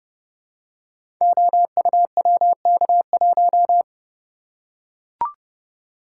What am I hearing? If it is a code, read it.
Morse code, heard as a keyed tone, about 30 wpm, 700 Hz. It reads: OUWK1
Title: Code Groups - 5 Letter:Numbers - 30wpm